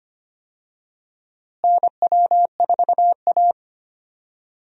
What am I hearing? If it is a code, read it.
Morse code, heard as a keyed tone, about 25 wpm, 700 Hz. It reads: NW4A
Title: USA Call Signs - 4 Characters - Repeated 3x - 25wpm